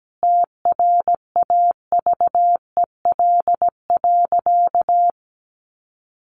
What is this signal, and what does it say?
Morse code, heard as a keyed tone, about 17 wpm, 700 Hz. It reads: TRAVEL.
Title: Sentences from Top 400 Words 17wpm